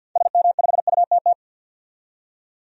Morse code, heard as a keyed tone, about 46 wpm, 700 Hz. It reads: SM5UTT
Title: DXCC-100 Call Signs - 6 Characters - 50wpm